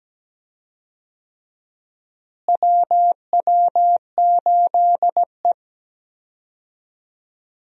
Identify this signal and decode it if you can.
Morse code, heard as a keyed tone, about 17 wpm, 700 Hz. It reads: WW8E
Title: USA Call Signs - 4 Characters - Repeated 3x - 17wpm